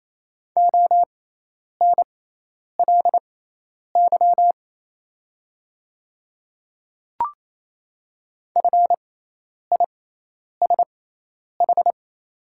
Morse code, heard as a keyed tone, about 28 wpm, spaced as for 10 wpm, 700 Hz. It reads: ONLY FISH
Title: 6x Character Spacing - Top 500 Words - 28wpm